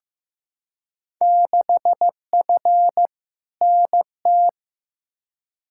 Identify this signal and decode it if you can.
Morse code, heard as a keyed tone, about 15 wpm, 700 Hz. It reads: 6F NT
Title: Field Day Exchanges Encore - 15wpm